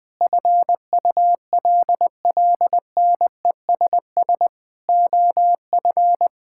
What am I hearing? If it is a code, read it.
Morse code, heard as a keyed tone, about 21 wpm, 700 Hz. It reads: FULLNESS OF
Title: Sets of 9 Words - 20wpm